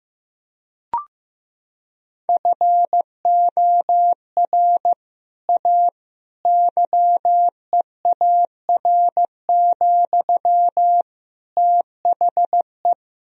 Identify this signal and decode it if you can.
Morse code, heard as a keyed tone, about 15 wpm, 700 Hz. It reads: FOR A YEAR, THE
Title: Sentences from Top 1000 Words - Review Part 1 - 15wpm